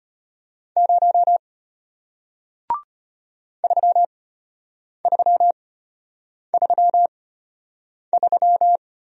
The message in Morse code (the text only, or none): 0 3333